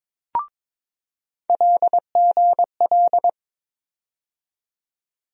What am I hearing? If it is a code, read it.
Morse code, heard as a keyed tone, about 22 wpm, 700 Hz. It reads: LGL
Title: USA Call Sign Postfixes - 3 Letters 22wpm